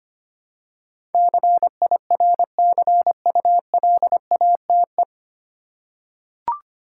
Morse code, heard as a keyed tone, about 25 wpm, 700 Hz. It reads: CIRCULATE
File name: Nine Letter Words - 25wpm